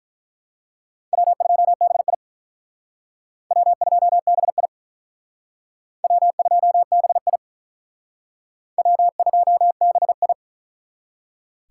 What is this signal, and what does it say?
Morse code, heard as a keyed tone, about 43 wpm, 700 Hz. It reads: W2BI W2BI W2BI W2BI
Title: Speed Racing - USA 1x2 Call Signs 35wpm